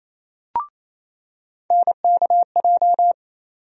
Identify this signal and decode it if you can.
Morse code, heard as a keyed tone, about 28 wpm, 700 Hz. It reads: NKJ
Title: USA Call Sign Postfixes - 3 Letters 28wpm